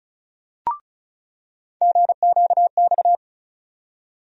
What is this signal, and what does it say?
Morse code, heard as a keyed tone, about 34 wpm, 700 Hz. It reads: GQX
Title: USA Call Sign Postfixes - 3 Letters 35wpm